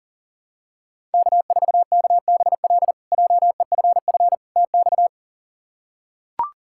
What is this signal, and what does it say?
Morse code, heard as a keyed tone, about 37 wpm, 700 Hz. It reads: K4KBL JEFF TX
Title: K1USN SST QSO Practice - 40wpm